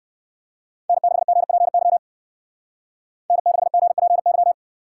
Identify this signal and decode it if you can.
Morse code, heard as a keyed tone, about 53 wpm, 700 Hz. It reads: N6GPY N6GPY
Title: Speed Racing - USA 1x3 Call Signs 45wpm